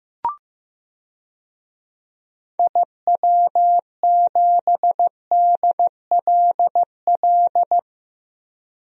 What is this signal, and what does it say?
Morse code, heard as a keyed tone, about 15 wpm, 700 Hz. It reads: IW7DLL
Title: DXCC-100 Call Signs - 6 Characters - Repeated 3x - 15wpm